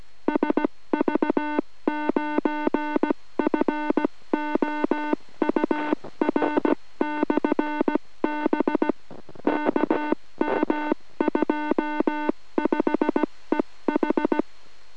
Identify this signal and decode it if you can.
Morse code, heard as a keyed tone, about 17 wpm, 3300 Hz, with noise. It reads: SV9FOUR/B KM25EH